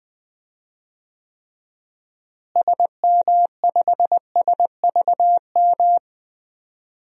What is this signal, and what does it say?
Morse code, heard as a keyed tone, about 20 wpm, 700 Hz. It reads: SM5SVM
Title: DXCC-100 Call Signs - 6 Characters - Repeated 2x - 20wpm